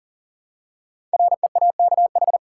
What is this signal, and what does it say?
Morse code, heard as a keyed tone, about 42 wpm, 690 Hz. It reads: REAKH
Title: Code Groups - 5 Letter:Numbers - 40wpm